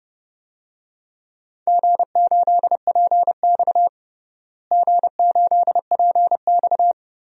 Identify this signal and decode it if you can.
Morse code, heard as a keyed tone, about 30 wpm, 700 Hz. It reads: G8PX G8PX
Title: DXCC-100 Call Signs - 4 Characters - Repeated 3x - 30wpm